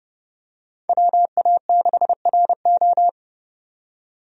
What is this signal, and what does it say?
Morse code, heard as a keyed tone, about 31 wpm, 700 Hz. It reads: WA6RO
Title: USA Call Signs - 5 Characters - 30wpm